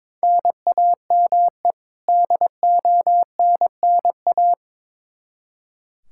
Morse code, heard as a keyed tone, about 22 wpm, 700 Hz. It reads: NAME DONNA
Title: Practice QSO - Exchange 1 - Beginning - 22wpm